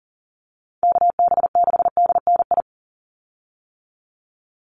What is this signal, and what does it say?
Morse code, heard as a keyed tone, about 40 wpm, 700 Hz. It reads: KB6DNI
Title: USA Call Signs 40wpm